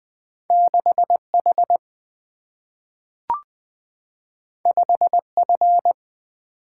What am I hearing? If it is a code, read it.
Morse code, heard as a keyed tone, about 20 wpm, 700 Hz. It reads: 6 H 5 F
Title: Single-Letter Single-Number 20wpm